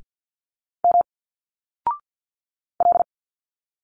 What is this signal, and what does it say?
Morse code, heard as a keyed tone, about 50 wpm, 700 Hz. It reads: M L
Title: Single Letters 50wpm